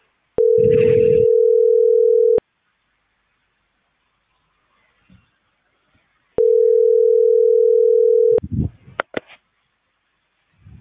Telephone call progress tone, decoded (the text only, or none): ringback tone